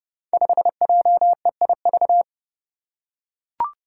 Morse code, heard as a keyed tone, about 30 wpm, 700 Hz. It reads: 5JEIV
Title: Five Challenging Letter-Number Combinations 30wpm